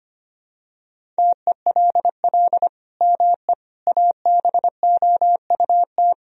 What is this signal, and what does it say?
Morse code, heard as a keyed tone, about 25 wpm, 700 Hz. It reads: TELL ME ABOUT
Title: Sentences from Top 1000 Words - Review Part 3 - 25wpm